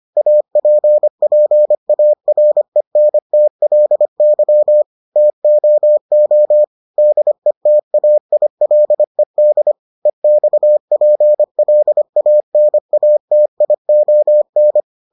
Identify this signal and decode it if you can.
Morse code, heard as a keyed tone, about 25 wpm, 600 Hz. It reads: APPARENTLY TOO DETAILED EXPLANATION